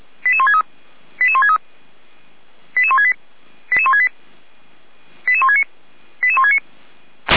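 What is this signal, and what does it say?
Five-tone selective call (ZVEI), heard as quick runs of five tones, tones of about 80 ms. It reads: 88163 88163 88168 88168 88169 88169
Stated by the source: SDR-received ZVEI five-tone selective call.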